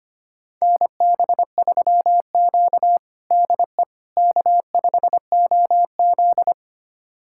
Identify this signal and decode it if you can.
Morse code, heard as a keyed tone, about 25 wpm, 700 Hz. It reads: NB3Q DE K5OZ
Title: USA Callsign DE Callsign - 4 Characters - 25wpm